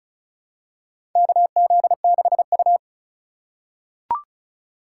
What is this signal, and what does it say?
Morse code, heard as a keyed tone, about 36 wpm, 700 Hz. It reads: KZ6U
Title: USA Call Signs - 4x Character Spacing - 35wpm